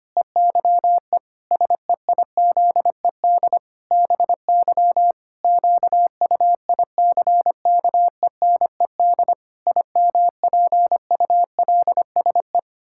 Morse code, heard as a keyed tone, about 25 wpm, 700 Hz. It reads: EYE SEIZED BY QUICKENED IMPULSE
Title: Pangram Challenge v2 - 25wpm